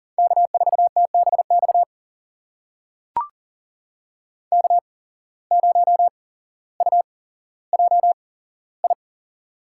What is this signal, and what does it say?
Morse code, heard as a keyed tone, about 38 wpm, spaced as for 17 wpm, 700 Hz. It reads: K4TBX K0UJI
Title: USA Call Signs - 5 Characters - 8x Character Spacing - 40wpm